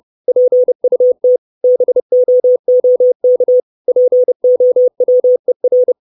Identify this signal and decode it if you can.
Morse code, heard as a keyed tone, about 30 wpm, 500 Hz. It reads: PUT BOOK POWER
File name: 632 Words 30WPM